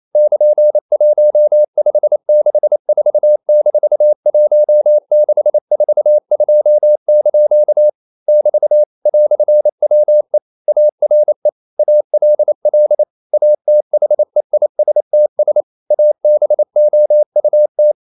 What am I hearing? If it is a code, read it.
Morse code, heard as a keyed tone, about 28 wpm, 600 Hz. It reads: <KN>1564-1642) <BT> "WE ARE ALL ATHEISTS ABOUT
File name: CW / Practice / EnglishQuotations0128